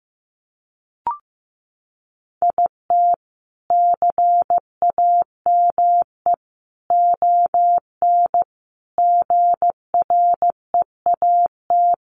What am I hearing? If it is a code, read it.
Morse code, heard as a keyed tone, about 14 wpm, 700 Hz. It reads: IT CAME ON GREAT
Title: A Sound Of Thunder 15wpm